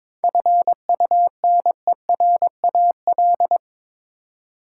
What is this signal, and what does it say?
Morse code, heard as a keyed tone, about 22 wpm, 700 Hz. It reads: FUNERAL
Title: Seven Letter Words - 22wpm